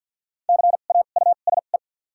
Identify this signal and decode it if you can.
Morse code, heard as a keyed tone, about 52 wpm, spaced as for 37 wpm, 700 Hz. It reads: CAUSE
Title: 2x Character Spacing - Top 200 Words - 50wpm